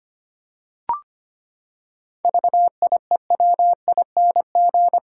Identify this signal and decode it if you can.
Morse code, heard as a keyed tone, about 25 wpm, 700 Hz. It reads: VIEWING
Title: Pangram Challenge v2 - 25wpm